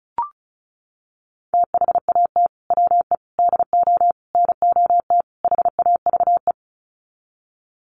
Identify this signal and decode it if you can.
Morse code, heard as a keyed tone, about 34 wpm, 700 Hz. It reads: THAT WE DO NOT HAVE
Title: Sets of 5 Words Encore 35wpm